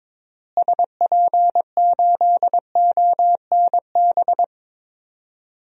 Morse code, heard as a keyed tone, about 22 wpm, 700 Hz. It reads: SP8ONB
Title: DXCC-100 Call Signs - 6 Characters - Repeated 3x - 22wpm